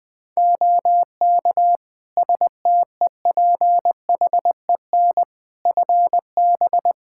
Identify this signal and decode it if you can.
Morse code, heard as a keyed tone, about 20 wpm, 700 Hz. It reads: OK STEPHEN FB
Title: Practice QSO - Exchange 2 - Info - 20wpm